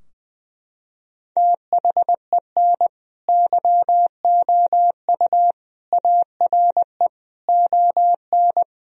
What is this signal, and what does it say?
Morse code, heard as a keyed tone, about 20 wpm, 700 Hz. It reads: THEN YOU ARE ON
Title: Sentences from Top 1000 Words - Review Part 4 - 20wpm